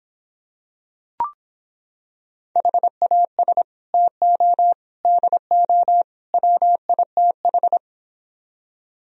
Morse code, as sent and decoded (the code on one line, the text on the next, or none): .... .- ... / - --- / -.. --- / .-- .. - ....
HAS TO DO WITH